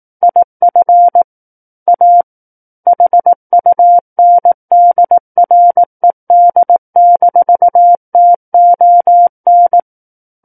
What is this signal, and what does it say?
Morse code, heard as a keyed tone, about 18 wpm, 700 Hz. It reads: IF A HUNDRED-TON
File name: Book One / 09-Chapter 9 - 18 WPM